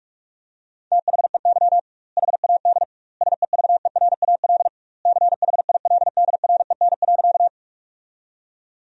Morse code, heard as a keyed tone, about 43 wpm, 700 Hz. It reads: THEY HAD SEVERAL CHILDREN.
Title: 2x Word-Spacing - Sentences from Top 400 Words - 45wpm